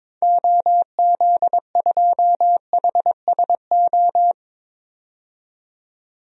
Morse code, heard as a keyed tone, about 22 wpm, 700 Hz. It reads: OZ2HSO
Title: DXCC-100 Call Signs - 6 Characters - 22wpm